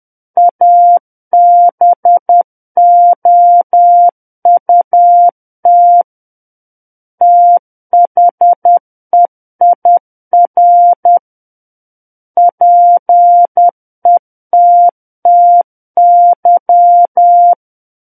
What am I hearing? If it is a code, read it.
Morse code, heard as a keyed tone, about 10 wpm, 700 Hz. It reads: ABOUT THEIR PETTY
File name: Book One / 01-Chapter 1 - 10 WPM